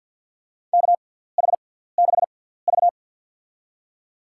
Morse code, heard as a keyed tone, about 53 wpm, spaced as for 17 wpm, 700 Hz. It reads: KH6V
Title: USA Call Signs - 4 Characters - 6x Character Spacing - 50wpm